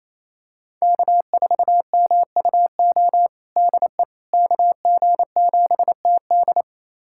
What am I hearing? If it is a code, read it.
Morse code, heard as a keyed tone, about 28 wpm, 700 Hz. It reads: K4MUO DE KG7TD
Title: USA Callsign DE Callsign - 5 Characters - 28wpm